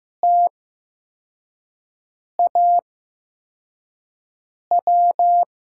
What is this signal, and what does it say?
Morse code, heard as a keyed tone, about 15 wpm, spaced as for 4 wpm, 700 Hz. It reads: TAW
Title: USA Call Signs - 5 Characters - 8x Character Spacing - 15wpm